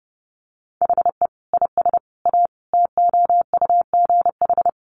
Morse code, heard as a keyed tone, about 30 wpm, 700 Hz. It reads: HE IS A TOUGH
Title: Sentences from Top 1200 Words 30wpm